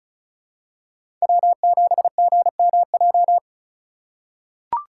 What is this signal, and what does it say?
Morse code, heard as a keyed tone, about 34 wpm, 700 Hz. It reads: W7GMJ
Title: USA Call Signs - 5 Characters - 2x Character Spacing - 35wpm